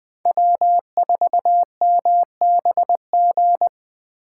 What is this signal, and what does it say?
Morse code, heard as a keyed tone, about 21 wpm, 700 Hz. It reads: W4MBG
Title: Speed Racing - USA 1x3 Call Signs 17wpm